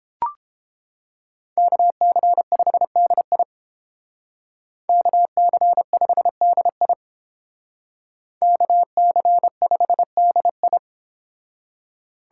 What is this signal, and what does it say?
Morse code, heard as a keyed tone, about 30 wpm, 700 Hz. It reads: KC5DI KC5DI KC5DI